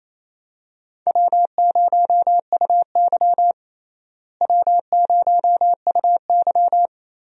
Morse code, heard as a keyed tone, about 28 wpm, 700 Hz. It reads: W0UY W0UY
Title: USA Call Signs - 4 Characters - Repeated 3x - 28wpm